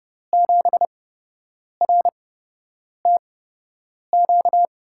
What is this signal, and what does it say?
Morse code, heard as a keyed tone, about 29 wpm, spaced as for 8 wpm, 700 Hz. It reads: 7RTQ
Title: USA Call Signs - 6 Characters - 8x Character Spacing - 30wpm